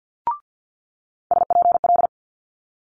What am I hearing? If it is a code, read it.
Morse code, heard as a keyed tone, about 50 wpm, 700 Hz. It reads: SPL
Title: Multi-letter Phonemes Quick Fire 50wpm